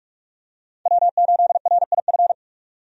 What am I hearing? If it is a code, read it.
Morse code, heard as a keyed tone, about 45 wpm, 700 Hz. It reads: W8RIF
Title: USA Call Signs - 5 Characters - 45wpm